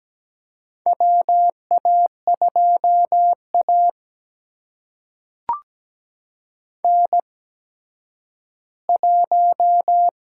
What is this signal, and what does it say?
Morse code, heard as a keyed tone, about 17 wpm, spaced as for 10 wpm, 700 Hz. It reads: WA2A N1
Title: USA Call Signs - 4 Characters - 8x Character Spacing - 17wpm